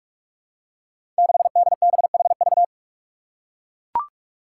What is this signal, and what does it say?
Morse code, heard as a keyed tone, about 44 wpm, 700 Hz. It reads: 6DBHV
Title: Five Challenging Letter-Number Combinations 45wpm